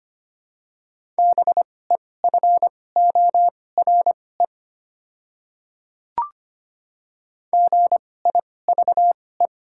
Morse code, heard as a keyed tone, about 25 wpm, spaced as for 18 wpm, 700 Hz. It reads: BEFORE GIVE
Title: 2x Character Spacing - Top 200 Words - 25wpm